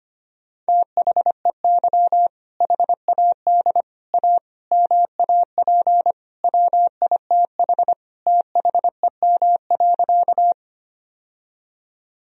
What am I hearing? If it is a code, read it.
Morse code, heard as a keyed tone, about 25 wpm, 700 Hz. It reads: THEY HAD A MAP WITH THEM.
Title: Sentences from Top 1000 Words - Review Part 2 - 25wpm